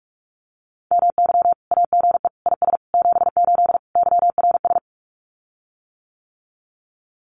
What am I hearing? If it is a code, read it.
Morse code, heard as a keyed tone, about 46 wpm, 700 Hz. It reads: MY AGE IS 78 YRS
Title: Common QSO Phrases Encore 45wpm